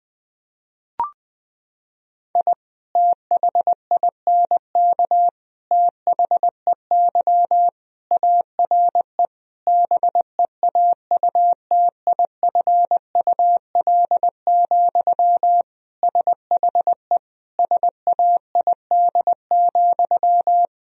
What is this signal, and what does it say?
Morse code, heard as a keyed tone, about 21 wpm, 700 Hz. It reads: I THINK THEY ARE BEAUTIFUL, SHE SAID,
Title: Sentences from Top 1000 Words - Review Part 4 - 20wpm